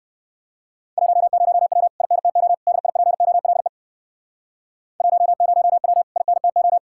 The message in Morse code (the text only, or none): 10W INTO DIPOLE 10W INTO